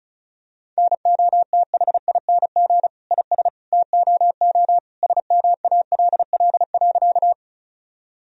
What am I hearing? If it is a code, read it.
Morse code, heard as a keyed tone, about 36 wpm, 700 Hz. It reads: NOTHING IS TOO SMALL.
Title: Sentences from Top 1000 Words - Review Part 1 - 35wpm